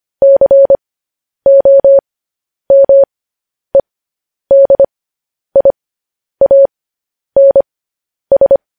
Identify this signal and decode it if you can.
Morse code, heard as a keyed tone, about 25 wpm, spaced as for 10 wpm, 550 Hz. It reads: COMEDIANS
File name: podcast / News headlines-1x-25x10